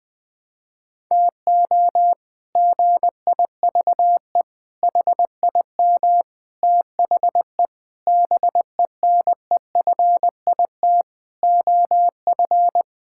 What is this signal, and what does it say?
Morse code, heard as a keyed tone, about 20 wpm, 700 Hz. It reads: TO GIVE HIM THE BENEFIT OF
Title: Sets of 8 Words - 20wpm